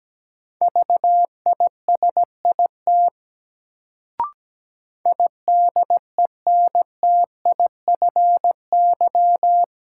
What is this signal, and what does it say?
Morse code, heard as a keyed tone, about 17 wpm, 700 Hz. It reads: VISIT IDENTIFY
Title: Sentences from Top 900 Words - New Word Drill - 17wpm